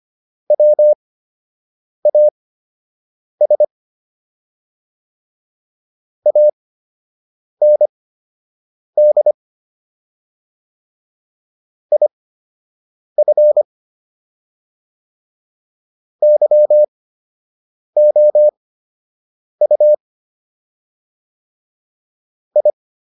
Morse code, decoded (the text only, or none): WAS AND IF YOU I